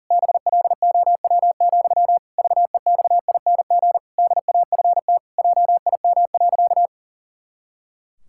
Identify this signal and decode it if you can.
Morse code, heard as a keyed tone, about 40 wpm, 700 Hz. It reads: BLOW, VEXING DAFT JIM.
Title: Pangram Challenge v2 - 40wpm